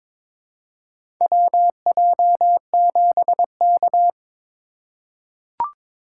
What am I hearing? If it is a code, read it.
Morse code, heard as a keyed tone, about 21 wpm, 700 Hz. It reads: WJ7K
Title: USA Call Signs - 4 Characters - 2x Character Spacing - 22wpm